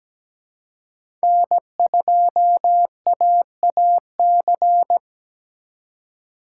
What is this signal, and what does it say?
Morse code, heard as a keyed tone, about 17 wpm, 700 Hz. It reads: N2AAC
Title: USA Call Signs - 5 Characters - 2x Character Spacing - 17wpm